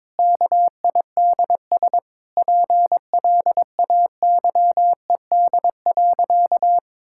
Morse code, heard as a keyed tone, about 22 wpm, 700 Hz. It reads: KIDS PLAYED.
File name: practice / 22wpm / Sentences with 3-Word Phrases - 22wpm